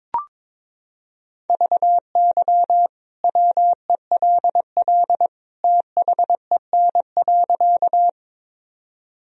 Morse code, heard as a keyed tone, about 22 wpm, 700 Hz. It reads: VY WELL THEN.
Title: Sentences from Top 200 Words and Abbreviations 22wpm